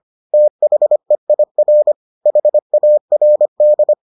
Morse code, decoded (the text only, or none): THEIR HARD